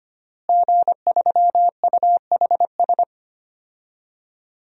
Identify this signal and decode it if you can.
Morse code, heard as a keyed tone, about 25 wpm, 700 Hz. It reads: G3UHS
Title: DXCC-100 Call Signs - 5 Characters - Repeated 2x - 25wpm